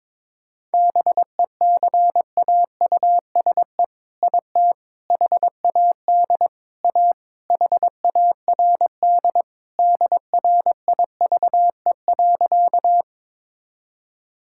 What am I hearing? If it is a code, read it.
Morse code, heard as a keyed tone, about 21 wpm, 700 Hz. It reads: BECAUSE IT HAD A HARD DRIVE.
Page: morse-code-ninja-practice-22wpm